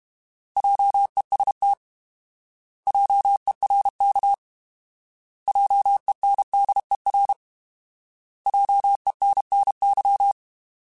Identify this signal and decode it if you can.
Morse code, heard as a keyed tone, about 32 wpm, 800 Hz. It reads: JEST JERK JENDER JENNY
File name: Alphabetical / Alphabetical mp3s / Js @30